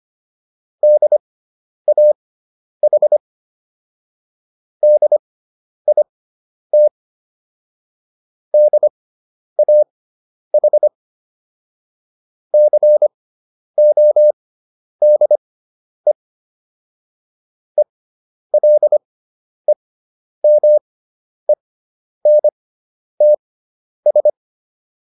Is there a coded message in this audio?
Yes: Morse code from 0.8 to 24.3 s, heard as a keyed tone, about 25 wpm, spaced as for 10 wpm, 600 Hz. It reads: DAH DIT DAH CODE ELEMENTS